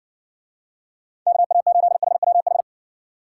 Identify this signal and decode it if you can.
Morse code, heard as a keyed tone, about 60 wpm, 700 Hz. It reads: KA8HWH